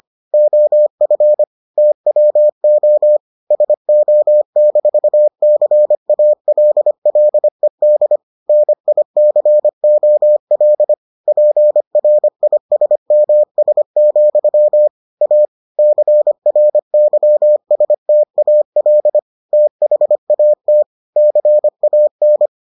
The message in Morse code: OF TWO SO-CALLED NICOL PRISMS, A CRYSTAL THAT CAN

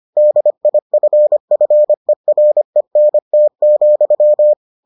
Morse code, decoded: DIFFERENT,

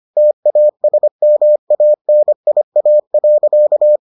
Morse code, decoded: TASMANIA.

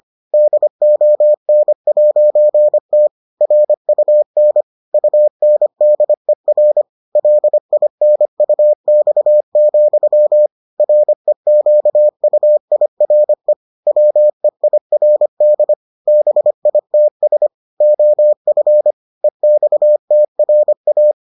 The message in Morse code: DON'T RUN UNDER LINUX, REQUIRE WEIRD BITS OF EXTRA

